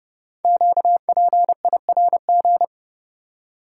Morse code, heard as a keyed tone, about 30 wpm, 700 Hz. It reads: QPIRG
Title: Code Groups - 5 Letters - 30wpm